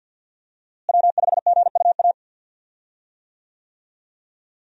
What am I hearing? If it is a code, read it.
Morse code, heard as a keyed tone, about 51 wpm, 700 Hz. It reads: W5GUA
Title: USA Call Signs - 5 Characters - Repeated 2x - 50wpm